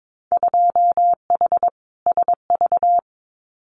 Morse code, heard as a keyed tone, about 22 wpm, 700 Hz. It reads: 2H SV
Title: Winter Field Day Exchanges Encore 22wpm